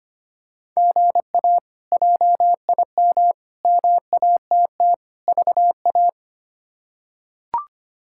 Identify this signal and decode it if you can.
Morse code, heard as a keyed tone, about 25 wpm, 700 Hz. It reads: GA JIM MATT VA
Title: K1USN SST QSO Practice - 25wpm